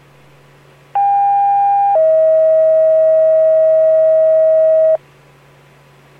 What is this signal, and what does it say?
Two-tone page, heard as a one-second tone, then a three-second tone, about 800 then 640 Hz. It is heard from 0.9 to 5.0 s.